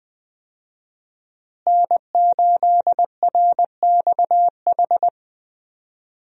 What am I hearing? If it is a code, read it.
Morse code, heard as a keyed tone, about 20 wpm, 700 Hz. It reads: N8RXH
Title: USA Call Signs - 5 Characters - Repeated 3x - 20wpm